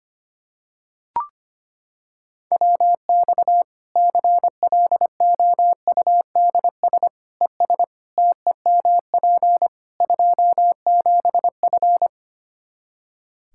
Morse code, heard as a keyed tone, about 25 wpm, 700 Hz. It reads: WX CLOUDS ES TEMP 27F
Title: Practice QSO - Exchange 2 - Info - 25wpm